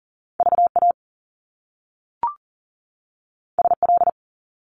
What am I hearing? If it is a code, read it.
Morse code, heard as a keyed tone, about 41 wpm, 700 Hz. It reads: VA SL
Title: Top Two Letter Combinations in Words 40wpm